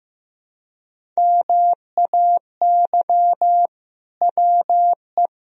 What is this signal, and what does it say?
Morse code, heard as a keyed tone, about 15 wpm, 700 Hz. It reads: MAY WE
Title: Sentences from Top 1000 Words - Review Part 1 - 15wpm